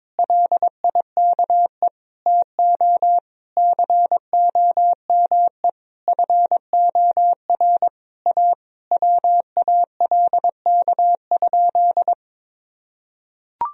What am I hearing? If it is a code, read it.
Morse code, heard as a keyed tone, about 21 wpm, 700 Hz. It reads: LIKE TO COME FOR A WALK?
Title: Sentences from Top 1000 Words - Review Part 1 - 22wpm